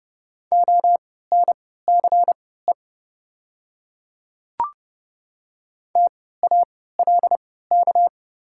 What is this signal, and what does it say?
Morse code, heard as a keyed tone, about 30 wpm, spaced as for 17 wpm, 700 Hz. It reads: ONCE TALK
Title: 3x Character Spacing - Top 500 Words - 30wpm